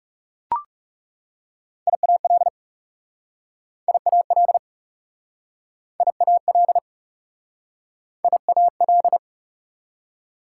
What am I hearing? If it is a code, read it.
Morse code, heard as a keyed tone, about 40 wpm, 700 Hz. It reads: IAL IAL IAL IAL